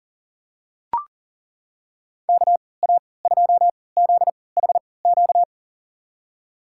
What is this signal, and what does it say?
Morse code, heard as a keyed tone, about 37 wpm, spaced as for 22 wpm, 690 Hz. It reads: KA2ZHQ
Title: USA Call Signs - 3x Character Spacing - 40wpm